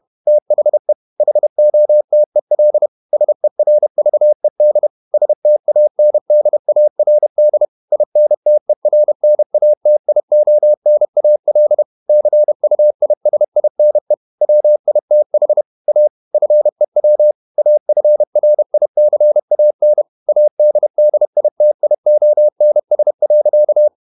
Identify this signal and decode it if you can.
Morse code, heard as a keyed tone, about 31 wpm, 600 Hz. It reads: THE HOTEL SERVED STANDARD INTERNATIONAL CUISINE WITH A FEW AFRICAN ADDITIONS.
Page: Never 31wpm | CW med Gnister